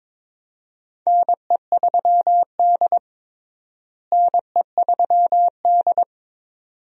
Morse code, heard as a keyed tone, about 22 wpm, 700 Hz. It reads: NE3D NE3D
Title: USA Call Signs - 4 Characters - Repeated 2x - 22wpm